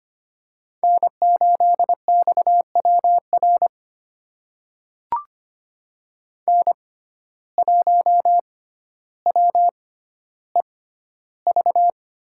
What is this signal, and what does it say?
Morse code, heard as a keyed tone, about 25 wpm, spaced as for 13 wpm, 700 Hz. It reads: N8XWR N1WEV
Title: USA Call Signs - 5 Characters - 6x Character Spacing - 25wpm